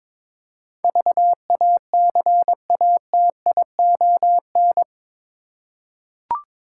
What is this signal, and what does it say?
Morse code, heard as a keyed tone, about 22 wpm, 700 Hz. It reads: VACATION
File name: practice / 22wpm / Eight Letter Words - 22wpm